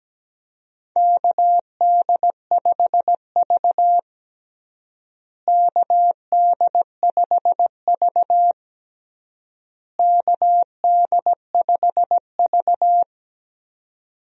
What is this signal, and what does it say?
Morse code, heard as a keyed tone, about 17 wpm, 700 Hz. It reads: KD5V KD5V KD5V
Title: USA Call Signs - 4 Characters - Repeated 3x - 17wpm